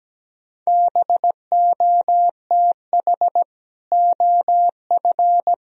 Morse code, heard as a keyed tone, about 17 wpm, 700 Hz. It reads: BOTH OF
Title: Sentences from Top 1000 Words - Review Part 2 - 17wpm